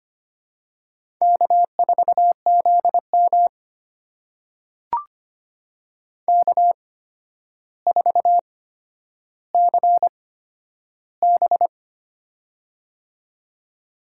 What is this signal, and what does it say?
Morse code, heard as a keyed tone, about 25 wpm, spaced as for 11 wpm, 700 Hz. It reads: K4ZM K4CB
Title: USA Call Signs - 4 Characters - 8x Character Spacing - 25wpm